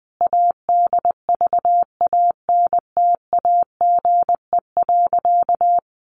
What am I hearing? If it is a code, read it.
Morse code, heard as a keyed tone, about 20 wpm, 700 Hz. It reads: ADVANTAGE.
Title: Sentences from Top 900 Words 20wpm